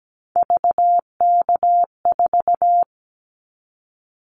Morse code, heard as a keyed tone, about 17 wpm, 700 Hz. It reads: VK4
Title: Callsign Prefixes for Top 100 DXCC - 17wpm